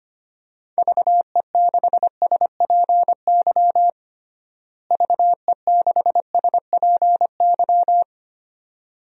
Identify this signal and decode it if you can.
Morse code, heard as a keyed tone, about 25 wpm, 700 Hz. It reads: VE6SPY VE6SPY
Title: DXCC-100 Call Signs - 6 Characters - Repeated 2x - 25wpm